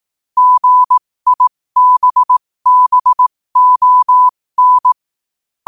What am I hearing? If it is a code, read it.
Morse code, heard as a keyed tone, about 18 wpm, spaced as for 15 wpm, 1000 Hz. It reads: GIBBON
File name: qotdfiles / 15wpm